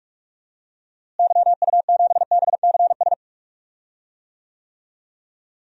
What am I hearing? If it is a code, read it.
Morse code, heard as a keyed tone, about 42 wpm, 700 Hz. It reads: YU7BCS
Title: DXCC-100 Call Signs - 6 Characters - Repeated 2x - 45wpm